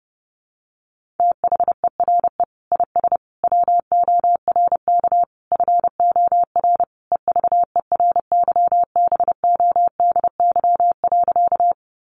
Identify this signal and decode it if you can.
Morse code, heard as a keyed tone, about 28 wpm, 700 Hz. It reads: THERE IS WORK FOR EVERYBODY.
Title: Sentences from Top 1400 Words 30wpm